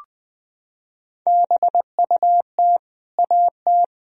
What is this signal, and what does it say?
Morse code, heard as a keyed tone, about 20 wpm, 700 Hz. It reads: BUT AT
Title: Sentences from Top 1000 Words - Review Part 3 - 20wpm